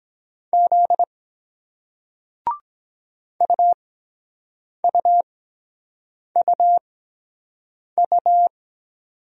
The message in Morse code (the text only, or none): Z UUUU